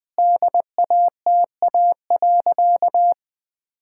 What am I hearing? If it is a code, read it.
Morse code, heard as a keyed tone, about 20 wpm, 700 Hz. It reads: DATA.
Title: Sentences with 6-Word Phrases - 20wpm